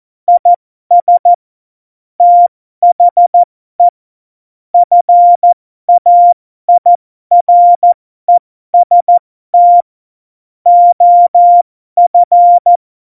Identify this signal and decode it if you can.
Morse code, heard as a keyed tone, about 14 wpm, spaced as for 12 wpm, 700 Hz. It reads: IS THE FAIREST OF
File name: Morsekurs / Witze